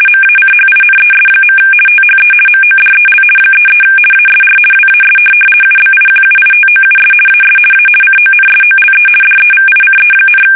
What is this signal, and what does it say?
RTTY, about 50 baud. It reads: MACHINES. THE MODEL 33 ASR WAS UBIQUITOUS AS A CONSOLE DEVICE,
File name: fsk850-50B / teletype-corp